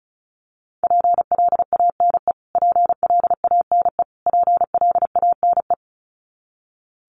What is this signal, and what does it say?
Morse code, heard as a keyed tone, about 34 wpm, 700 Hz. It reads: PLANE PLANE PLANE
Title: Top 500 Words Repeated 35wpm